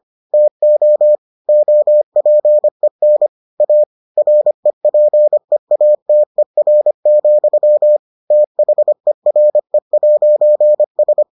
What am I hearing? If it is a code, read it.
Morse code, heard as a keyed tone, about 25 wpm, 600 Hz. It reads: TO OPEN A REPEATER, THERE'S